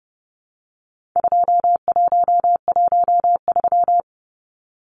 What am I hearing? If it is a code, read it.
Morse code, heard as a keyed tone, about 30 wpm, 700 Hz. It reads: 2113
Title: Numbers - 4-Digit 30wpm